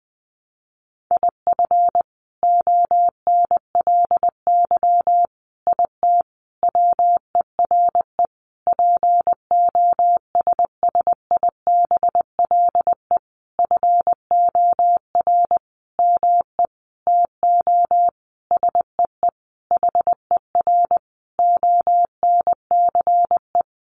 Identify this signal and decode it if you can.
Morse code, heard as a keyed tone, about 19 wpm, 700 Hz. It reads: IF ONLY IT WERE POSSIBLE FOR ME TO SEE HER ONCE
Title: Sentences from Top 500 Words 20wpm